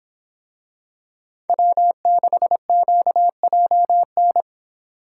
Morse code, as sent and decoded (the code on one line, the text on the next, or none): .-- -.... --.- .--- -.
W6QJN